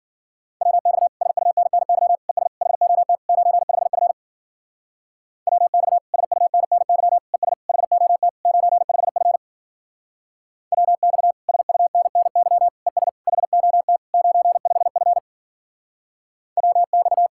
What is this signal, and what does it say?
Morse code, heard as a keyed tone, about 50 wpm, 700 Hz. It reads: WX SUNNY ES HOT 95F WX SUNNY ES HOT 95F WX SUNNY ES HOT 95F WX